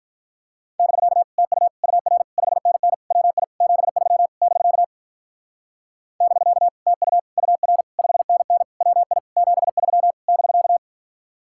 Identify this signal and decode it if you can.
Morse code, heard as a keyed tone, about 45 wpm, 700 Hz. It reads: <BK> TU UR 5NN WI 73 <BK> <BK> TU UR 5NN WI 73 <BK>